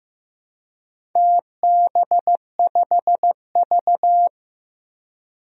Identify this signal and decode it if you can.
Morse code, heard as a keyed tone, about 15 wpm, 700 Hz. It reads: TB5V
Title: DXCC-100 Call Signs - 4 Characters - 15wpm